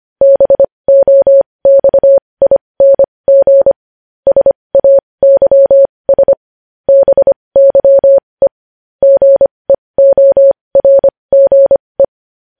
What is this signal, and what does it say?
Morse code, heard as a keyed tone, about 25 wpm, spaced as for 20 wpm, 550 Hz. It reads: BOXING SAYS BYE GEORGE
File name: podcast / News headlines-1x-25x20